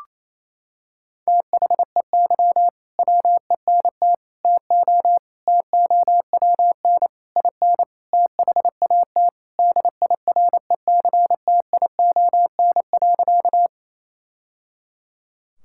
Morse code, heard as a keyed tone, about 27 wpm, 700 Hz. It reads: THEY WENT TO TOWN IN THAT DIRECTION.
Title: Sentences from Top 1000 Words - Review Part 5 - 28wpm